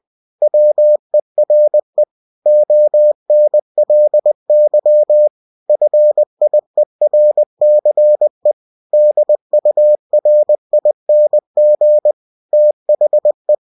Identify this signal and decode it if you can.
Morse code, heard as a keyed tone, about 20 wpm, 600 Hz. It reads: WERE ONLY FIERCE DURING THE
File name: cw-books / winnie-the-pooh / winnie-20wpm-page-0062